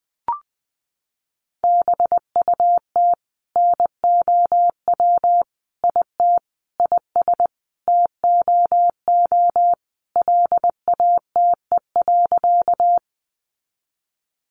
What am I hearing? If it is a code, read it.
Morse code, heard as a keyed tone, about 20 wpm, 700 Hz. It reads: BUT NOW IT IS TOO LATE.
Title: Sentences from Top 300 Words 20wpm